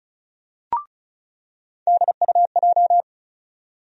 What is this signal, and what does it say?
Morse code, heard as a keyed tone, about 34 wpm, 700 Hz. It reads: DUJ
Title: USA Call Sign Postfixes - 3 Letters 35wpm